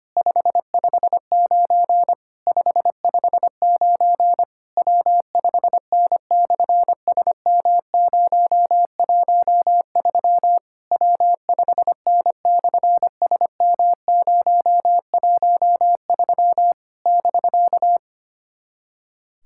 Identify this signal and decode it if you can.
Morse code, heard as a keyed tone, about 25 wpm, 700 Hz. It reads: 559 559 W5N/SM013 W5N/SM013 <BK>
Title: SOTA QSO Practice 25wpm